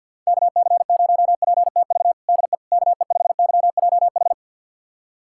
Morse code, heard as a keyed tone, about 51 wpm, 700 Hz. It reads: KC0PTV DE KE5YJH
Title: USA Callsign DE Callsign 50wpm